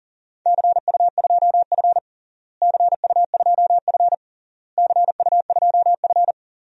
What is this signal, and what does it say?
Morse code, heard as a keyed tone, about 40 wpm, 700 Hz. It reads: CU2F CU2F CU2F
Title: DXCC-100 Call Signs - 4 Characters - Repeated 3x - 40wpm